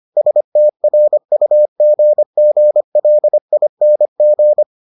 Morse code, heard as a keyed tone, about 25 wpm, 600 Hz. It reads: STRUGGLING